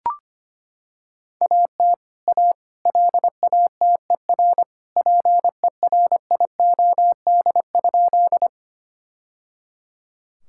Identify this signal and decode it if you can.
Morse code, heard as a keyed tone, about 25 wpm, 700 Hz. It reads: AT A LATER PERIOD?
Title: Sentences from Top 1000 Words - Review Part 5 - 25wpm